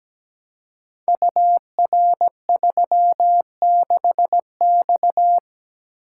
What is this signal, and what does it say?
Morse code, heard as a keyed tone, about 17 wpm, 700 Hz. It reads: UR36X
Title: Code Groups - 5 Letter:Numbers - 17wpm